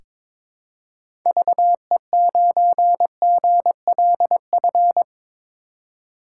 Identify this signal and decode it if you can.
Morse code, heard as a keyed tone, about 22 wpm, 700 Hz. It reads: VE9GLF
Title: K1USN SST QSO Practice - 22wpm